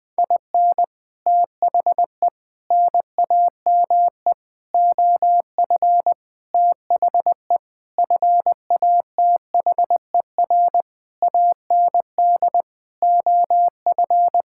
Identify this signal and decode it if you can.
Morse code, heard as a keyed tone, about 21 wpm, 700 Hz. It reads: IN THE NAME OF THE FATHER AND OF
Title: Sentences with 9-Word Phrases - 20wpm